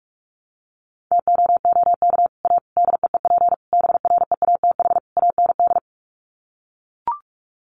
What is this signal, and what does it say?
Morse code, heard as a keyed tone, about 44 wpm, 700 Hz. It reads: TOOK A DEEP BREATH AND
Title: Sets of 5 Words Encore 45wpm